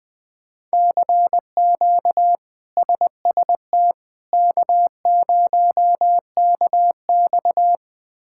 Morse code, heard as a keyed tone, about 20 wpm, 700 Hz. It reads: CQ SST K0KX
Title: K1USN SST QSO Practice - 20wpm